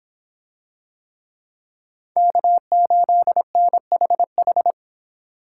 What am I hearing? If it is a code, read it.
Morse code, heard as a keyed tone, about 26 wpm, 700 Hz. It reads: K8NHH